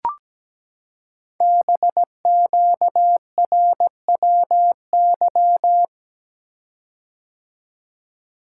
Morse code, heard as a keyed tone, about 17 wpm, 700 Hz. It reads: BQRWY
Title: Code Groups - 5 Letters - 17wpm